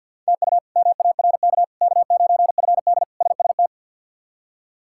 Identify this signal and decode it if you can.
Morse code, heard as a keyed tone, about 49 wpm, 700 Hz. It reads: TU MARK K9FD SST
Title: K1USN SST QSO Practice - 50wpm